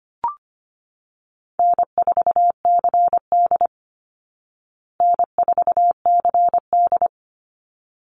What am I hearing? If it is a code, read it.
Morse code, heard as a keyed tone, about 25 wpm, 700 Hz. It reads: N4CD N4CD
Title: Easy Call Signs Repeated 2x - 25wpm